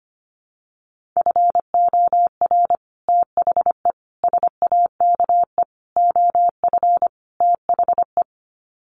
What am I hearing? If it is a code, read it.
Morse code, heard as a keyed tone, about 25 wpm, 700 Hz. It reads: FOR THE SAKE OF THE
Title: Sets of 5 Words Encore 25wpm